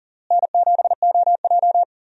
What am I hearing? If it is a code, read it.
Morse code, heard as a keyed tone, about 40 wpm, 700 Hz. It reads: N7OJ
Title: USA Call Signs - 4 Characters - Repeated 2x - 40wpm